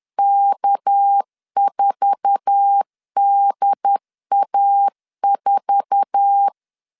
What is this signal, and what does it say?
Morse code, heard as a keyed tone, about 10 wpm, 800 Hz. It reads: K4DA4
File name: LEZIONI / 8 YJ45 / 8 YJ4 10wpm 2633